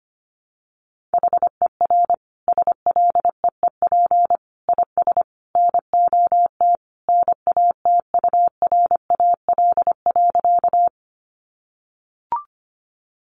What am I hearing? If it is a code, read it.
Morse code, heard as a keyed tone, about 25 wpm, 700 Hz. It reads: HER SLEEP IS NOT NATURAL.
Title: Sentences from Top 700 Words 25wpm